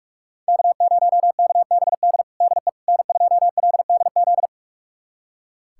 Morse code, heard as a keyed tone, about 41 wpm, 700 Hz. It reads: K0KBD DE N2LDZ
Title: USA Callsign DE Callsign - 5 Characters - 45wpm